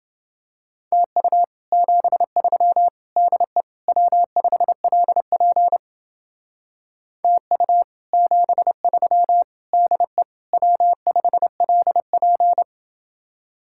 Morse code, heard as a keyed tone, about 29 wpm, 700 Hz. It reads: TU 73 DE W5LP TU 73 DE W5LP